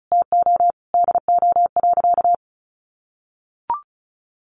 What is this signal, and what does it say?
Morse code, heard as a keyed tone, about 34 wpm, 700 Hz. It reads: TO DO.
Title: Sentences from Top 600 Words 35wpm